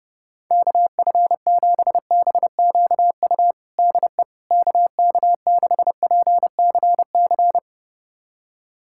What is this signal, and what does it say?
Morse code, heard as a keyed tone, about 32 wpm, 700 Hz. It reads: KF7BQU DE KK6PCC
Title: USA Callsign DE Callsign 30wpm